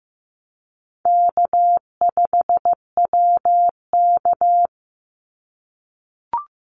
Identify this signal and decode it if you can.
Morse code, heard as a keyed tone, about 15 wpm, 700 Hz. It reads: K5WK
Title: Easy Call Signs - 15wpm